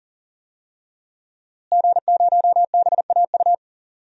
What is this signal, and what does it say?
Morse code, heard as a keyed tone, about 40 wpm, 700 Hz. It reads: G0BAU
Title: DXCC-100 Call Signs - 5 Characters - Repeated 3x - 40wpm